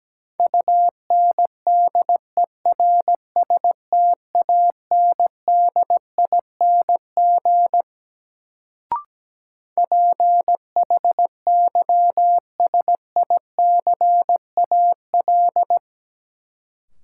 Morse code, heard as a keyed tone, about 17 wpm, 700 Hz. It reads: UNDERSTANDING PHYSICAL
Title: Sentences from Top 1400 Words - New Word Drill Alt - 17wpm